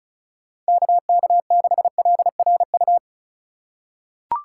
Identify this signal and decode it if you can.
Morse code, heard as a keyed tone, about 36 wpm, 700 Hz. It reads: KK6LRU
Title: USA Call Signs - 6 Characters - 2x Character Spacing - 35wpm